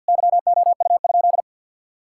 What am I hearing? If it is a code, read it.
Morse code, heard as a keyed tone, about 52 wpm, 700 Hz. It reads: YOU?
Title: Sentences from Top 1000 Words - Review Part 5 - 50wpm